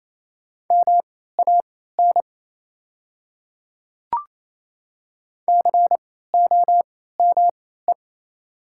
Morse code, heard as a keyed tone, about 28 wpm, spaced as for 16 wpm, 700 Hz. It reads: MAN COME
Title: 3x Character Spacing - Top 300 Words - 28wpm